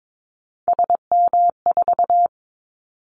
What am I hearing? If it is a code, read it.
Morse code, heard as a keyed tone, about 22 wpm, 700 Hz. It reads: SM4
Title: Callsign Prefixes for Top 25 DXCC - 22wpm